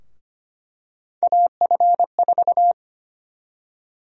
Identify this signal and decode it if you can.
Morse code, heard as a keyed tone, about 25 wpm, 700 Hz. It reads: AF4
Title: USA Call Sign Prefixes - Repeated 2x - 25wpm